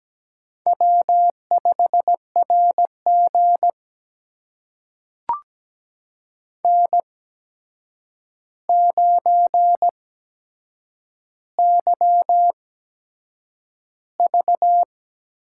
Morse code, heard as a keyed tone, about 17 wpm, spaced as for 7 wpm, 700 Hz. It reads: W5RG N9YV
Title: USA Call Signs - 4 Characters - 8x Character Spacing - 17wpm